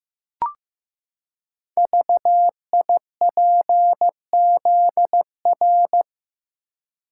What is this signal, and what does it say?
Morse code, heard as a keyed tone, about 15 wpm, 700 Hz. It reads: VIPZR
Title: Code Groups - 5 Letters - 15wpm